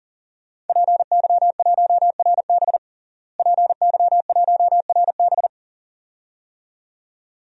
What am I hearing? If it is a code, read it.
Morse code, heard as a keyed tone, about 40 wpm, 700 Hz. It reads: PY1RB PY1RB
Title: DXCC-100 Call Signs - 5 Characters - Repeated 2x - 40wpm